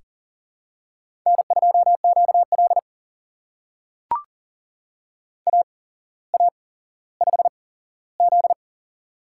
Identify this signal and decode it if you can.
Morse code, heard as a keyed tone, about 44 wpm, spaced as for 17 wpm, 700 Hz. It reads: N2QL AA5Z
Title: USA Call Signs - 4 Characters - 8x Character Spacing - 40wpm